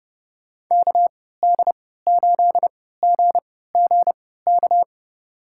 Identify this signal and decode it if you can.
Morse code, heard as a keyed tone, about 30 wpm, spaced as for 17 wpm, 700 Hz. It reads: KD8GGK
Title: USA Call Signs - 6 Characters - 3x Character Spacing - 30wpm